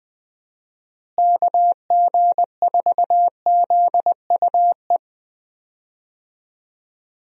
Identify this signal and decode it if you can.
Morse code, heard as a keyed tone, about 20 wpm, 700 Hz. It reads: KG4ZUE
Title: USA Call Signs - 6 Characters - Repeated 2x - 20wpm